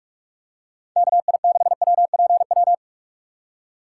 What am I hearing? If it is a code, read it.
Morse code, heard as a keyed tone, about 46 wpm, 700 Hz. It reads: KI6WPW
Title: USA Call Signs - 6 Characters - 4x Character Spacing - 45wpm